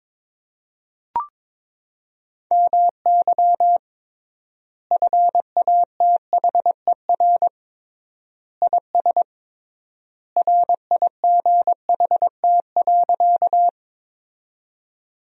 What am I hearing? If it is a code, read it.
Morse code, heard as a keyed tone, about 22 wpm, 700 Hz. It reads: MY FATHER IS RIGHT.
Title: 3x Word-Spacing - Sentences from Top 300 Words - 22wpm